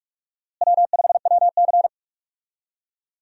Morse code, heard as a keyed tone, about 46 wpm, 700 Hz. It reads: W5WC
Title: USA Call Signs - 4 Characters - Repeated 3x - 45wpm